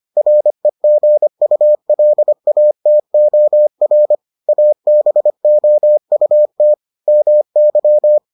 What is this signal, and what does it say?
Morse code, heard as a keyed tone, about 25 wpm, 600 Hz. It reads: REGULATOR ABOUT MY